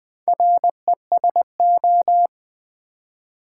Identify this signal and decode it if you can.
Morse code, heard as a keyed tone, about 20 wpm, 700 Hz. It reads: RESO
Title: Build A Word - 4 Letters At A Time - 10-Letter Words - 20wpm